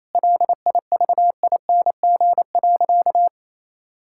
Morse code, heard as a keyed tone, about 28 wpm, 700 Hz. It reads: LIVING.
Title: Sentences from Top 1000 Words - Review Part 6 - 28wpm